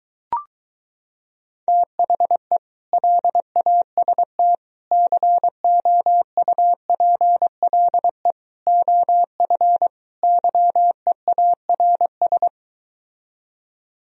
Morse code, heard as a keyed tone, about 23 wpm, 700 Hz. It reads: THE LAST COUPLE OF YEARS